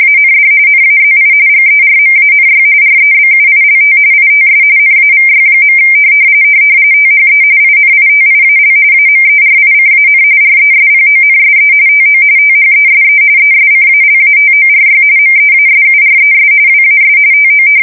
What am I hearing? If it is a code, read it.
RTTY, about 45.45 baud. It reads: AND NAZI GERMANY REBUILT ITS NAVY. THE INITIAL AMERICAN RESPONSE TO THESE ESCALATORY MEASURES WAS TO
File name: fsk170 / battleshipnj